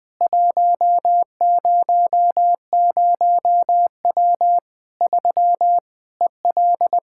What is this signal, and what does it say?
Morse code, heard as a keyed tone, about 20 wpm, 700 Hz. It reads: 100W 3 EL
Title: Practice QSO - Exchange 2 - Info - 20wpm